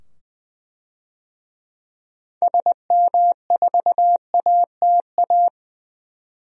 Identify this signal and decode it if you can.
Morse code, heard as a keyed tone, about 20 wpm, 700 Hz. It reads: SM4ATA
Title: DXCC-100 Call Signs - 6 Characters - 20wpm